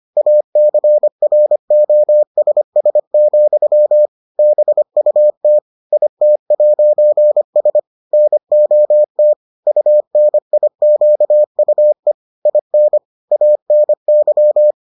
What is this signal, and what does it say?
Morse code, heard as a keyed tone, about 25 wpm, 600 Hz. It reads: ACROSS, BUT IT'S NOT UNIQUE IN ANY